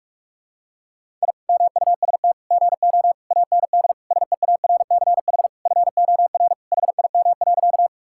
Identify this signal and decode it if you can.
Morse code, heard as a keyed tone, about 41 wpm, 700 Hz. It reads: I MUST GO AND SEARCH FOR HIM.
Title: Sentences from Top 1000 Words - Review Part 5 - 45wpm